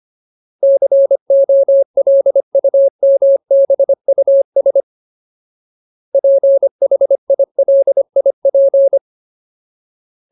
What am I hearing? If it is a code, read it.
Morse code, heard as a keyed tone, about 25 wpm, 550 Hz. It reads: COLUMBUS PHILIP
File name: Ultimate Word List / 25WPM Ultimate